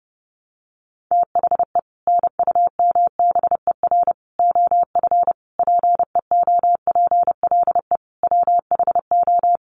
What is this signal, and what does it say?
Morse code, heard as a keyed tone, about 30 wpm, 700 Hz. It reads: THE NUMBER OF PEOPLE WHO
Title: Sets of 5 Words Encore 30wpm